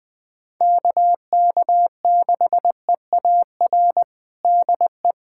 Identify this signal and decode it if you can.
Morse code, heard as a keyed tone, about 20 wpm, 700 Hz. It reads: KK6EAR DE
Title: USA Callsign DE Callsign 20wpm